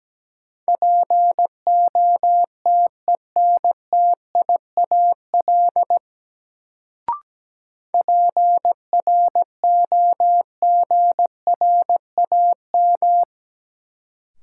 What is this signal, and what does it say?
Morse code, heard as a keyed tone, about 17 wpm, 700 Hz. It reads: POTENTIAL PROGRAM
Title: Sentences from Top 1000 Words - Review Part 4 - New Word Drill - 17wpm